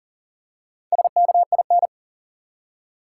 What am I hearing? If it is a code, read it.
Morse code, heard as a keyed tone, about 40 wpm, 700 Hz. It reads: SKIN
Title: Four Letter Words - 40wpm